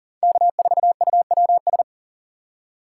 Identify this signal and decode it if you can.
Morse code, heard as a keyed tone, about 37 wpm, 700 Hz. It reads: K4UWS
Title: USA Call Signs - 5 Characters - 2x Character Spacing - 40wpm